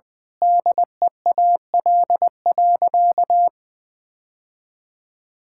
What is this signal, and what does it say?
Morse code, heard as a keyed tone, about 20 wpm, 700 Hz. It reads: DEAL.
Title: Sentences with 7-Word Phrases - 20wpm